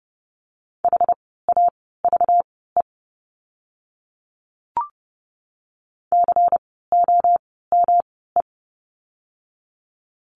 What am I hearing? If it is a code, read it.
Morse code, heard as a keyed tone, about 30 wpm, spaced as for 17 wpm, 700 Hz. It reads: HAVE COME
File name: course / 30:10wpm / 237 Lesson 38f - Letter q - Words 30wpm